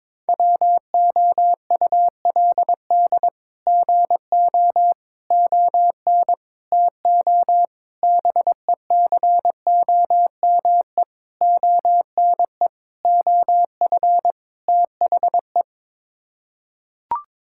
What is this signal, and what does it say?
Morse code, heard as a keyed tone, about 22 wpm, 700 Hz. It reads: WOULD GO ON TO BECOME ONE OF THE
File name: practice / 22wpm / Sets of 8 Words - 22wpm